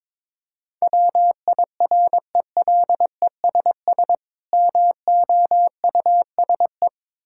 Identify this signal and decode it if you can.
Morse code, heard as a keyed tone, about 23 wpm, 690 Hz. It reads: WIRELESS MOUSE
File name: Difficult Words with Hints - 22wpm